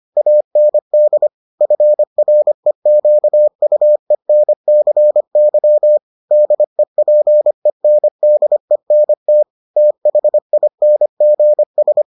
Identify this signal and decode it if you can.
Morse code, heard as a keyed tone, about 25 wpm, 600 Hz. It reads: AND FREQUENCY DEPENDENT THINGS